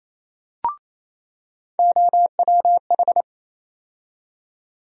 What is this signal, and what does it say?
Morse code, heard as a keyed tone, about 28 wpm, 700 Hz. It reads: OWH
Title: USA Call Sign Postfixes - 3 Letters 28wpm